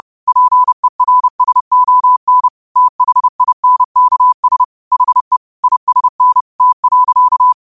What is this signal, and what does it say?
Morse code, heard as a keyed tone, about 30 wpm, 1000 Hz. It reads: PERSON THINKS HE ISN T.
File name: qotdfiles / 30wpm